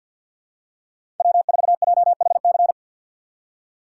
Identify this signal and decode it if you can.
Morse code, heard as a keyed tone, about 53 wpm, 700 Hz. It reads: W4JHC
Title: USA Call Signs - 5 Characters - 6x Character Spacing - 50wpm